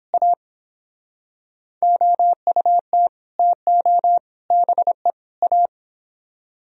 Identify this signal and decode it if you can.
Morse code, heard as a keyed tone, about 26 wpm, 700 Hz. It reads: A OUT TO BE A
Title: Speed-Racing - Sets of 4 Words 22wpm